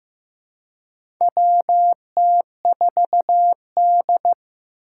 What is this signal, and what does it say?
Morse code, heard as a keyed tone, about 15 wpm, 700 Hz. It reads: WT4D
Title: USA Call Signs - 4 Characters - 4x Character Spacing - 15wpm